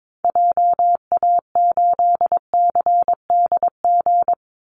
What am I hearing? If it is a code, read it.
Morse code, heard as a keyed tone, about 22 wpm, 700 Hz. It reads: JA8CDG
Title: Call Signs Repeated 2x - 22wpm